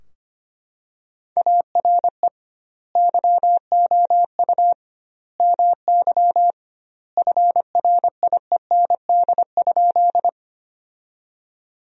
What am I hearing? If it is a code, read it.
Morse code, heard as a keyed tone, about 25 wpm, 700 Hz. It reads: ARE YOU MY FRIEND?
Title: 2x Word-Spacing - Sentences from Top 300 Words - 25wpm